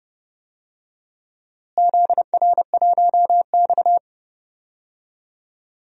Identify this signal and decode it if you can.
Morse code, heard as a keyed tone, about 29 wpm, 700 Hz. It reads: ZR1X
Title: DXCC-100 Call Signs - 4 Characters - Repeated 3x - 30wpm